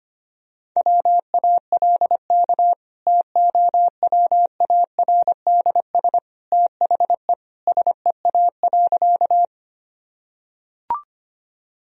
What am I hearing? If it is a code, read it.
Morse code, heard as a keyed tone, about 25 wpm, 700 Hz. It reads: WALK TOWARDS THE SEA.
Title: Sentences from Top 1000 Words - Review Part 2 - 25wpm